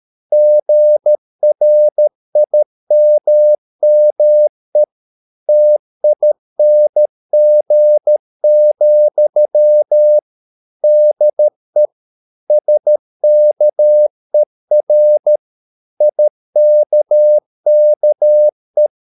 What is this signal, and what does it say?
Morse code, heard as a keyed tone, about 13 wpm, 600 Hz. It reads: GRIMME TING, DE SKER IKKE
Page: Ferie 13 wpm